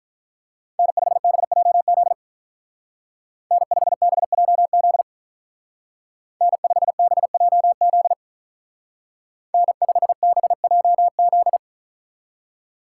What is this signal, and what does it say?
Morse code, heard as a keyed tone, about 44 wpm, 700 Hz. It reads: N5BJZ N5BJZ N5BJZ N5BJZ